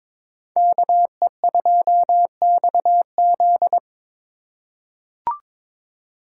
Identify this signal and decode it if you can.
Morse code, heard as a keyed tone, about 22 wpm, 700 Hz. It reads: KE2XZ
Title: USA Call Signs - 3x Character Spacing - 22wpm